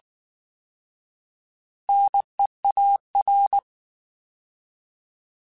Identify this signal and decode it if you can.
Morse code, heard as a keyed tone, about 19 wpm, 800 Hz. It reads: NEAR
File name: Alphabetical / Alphabetical mp3s / Ns @18